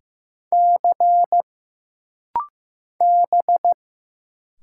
Morse code, heard as a keyed tone, about 15 wpm, 700 Hz. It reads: C B
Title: Single Letters - Rapid-Fire 15wpm